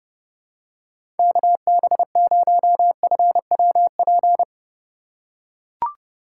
Morse code, heard as a keyed tone, about 29 wpm, 700 Hz. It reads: KB0FWP
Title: USA Call Signs - 6 Characters - 6x Character Spacing - 30wpm